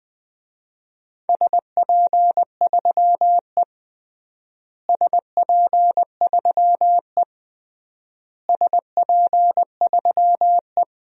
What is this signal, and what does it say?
Morse code, heard as a keyed tone, about 20 wpm, 700 Hz. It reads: SP3E SP3E SP3E
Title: DXCC-100 Call Signs - 4 Characters - Repeated 3x - 20wpm